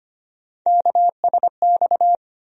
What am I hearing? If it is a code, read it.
Morse code, heard as a keyed tone, about 25 wpm, 700 Hz. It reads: KSX
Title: USA Call Sign Postfixes - 3 Letters 25wpm